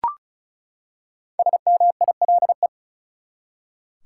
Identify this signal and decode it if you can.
Morse code, heard as a keyed tone, about 36 wpm, 700 Hz. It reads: SMILE
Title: Five Letter Words - 35wpm